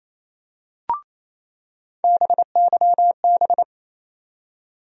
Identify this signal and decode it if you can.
Morse code, heard as a keyed tone, about 28 wpm, 700 Hz. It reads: BYB
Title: USA Call Sign Postfixes - 3 Letters 28wpm